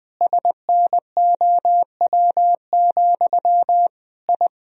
Morse code, heard as a keyed tone, about 21 wpm, 700 Hz. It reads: SNOW, I
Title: Sentences from Top 1000 Words - Review Part 3 - 20wpm